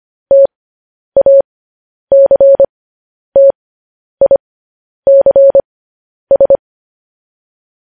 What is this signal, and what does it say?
Morse code, heard as a keyed tone, about 25 wpm, spaced as for 10 wpm, 550 Hz. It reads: TACTICS
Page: news-headlines-in-morse-code-at-10-wpm